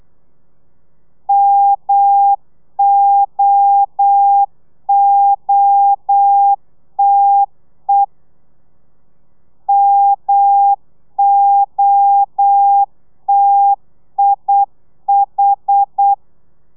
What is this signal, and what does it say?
Morse code, heard as a keyed tone, about 8 wpm, 800 Hz. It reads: MOOTE MOTIH
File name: telegrafia / i2tfj / Lez1 35